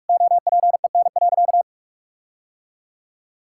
Morse code, heard as a keyed tone, about 46 wpm, 700 Hz. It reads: OPEN.
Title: Sentences from Top 1600 Words 45wpm